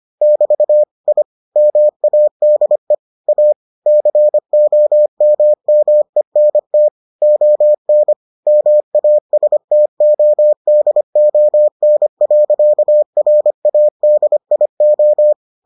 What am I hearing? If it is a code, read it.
Morse code, heard as a keyed tone, about 25 wpm, 600 Hz. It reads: <BT> I MADE A COMMENT ON MASTODON.RADIO